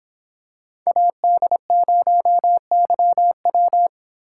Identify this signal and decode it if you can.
Morse code, heard as a keyed tone, about 26 wpm, 700 Hz. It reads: AD0YW
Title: Speed Racing - USA 2x2 Call Signs 22wpm